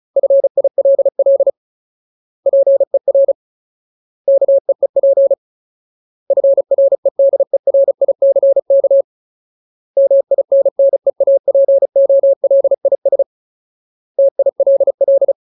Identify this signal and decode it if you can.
Morse code, heard as a keyed tone, about 35 wpm, 550 Hz. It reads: FILL PER KEEP FREDERICK MINNEAPOLIS TILL
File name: Ultimate Word List / 35WPM Ultimate